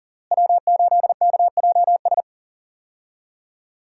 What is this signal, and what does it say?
Morse code, heard as a keyed tone, about 40 wpm, 700 Hz. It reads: W8KJS
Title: USA Call Signs - 5 Characters - Repeated 3x - 40wpm